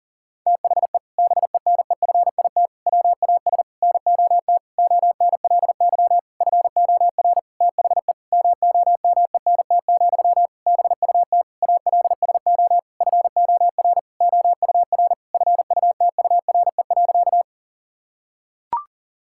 Morse code, heard as a keyed tone, about 40 wpm, 700 Hz. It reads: THE BENEFIT WAS NOT ONLY FOR THE MOMENT, BUT ALSO FOR OUR FUTURE.
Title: Sentences from Top 1000 Words - Review Part 6 - 40wpm